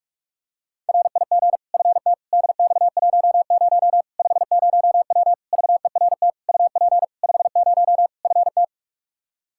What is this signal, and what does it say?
Morse code, heard as a keyed tone, about 46 wpm, 700 Hz. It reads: RIG FT DX10 50W VERT UP 50 FT
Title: Practice QSO - Exchange 2 - Info - 45wpm